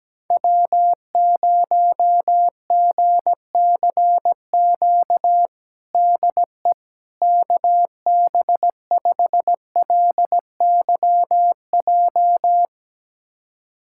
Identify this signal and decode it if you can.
Morse code, heard as a keyed tone, about 17 wpm, 700 Hz. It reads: W0GCQ DE KB5LYJ
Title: USA Callsign DE Callsign 17wpm